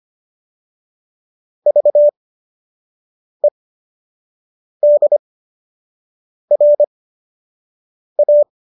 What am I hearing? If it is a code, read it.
Morse code, heard as a keyed tone, about 25 wpm, spaced as for 6 wpm, 590 Hz. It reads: VEDRA